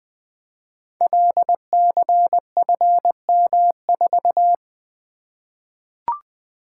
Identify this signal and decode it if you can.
Morse code, heard as a keyed tone, about 20 wpm, 700 Hz. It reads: LCFM4
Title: Code Groups - 5 Letter:Numbers - 20wpm